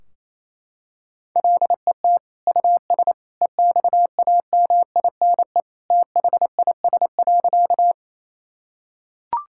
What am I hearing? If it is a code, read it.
Morse code, heard as a keyed tone, about 28 wpm, 700 Hz. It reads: LET US EXAMINE THIS.
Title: Sentences from Top 1600 Words 28wpm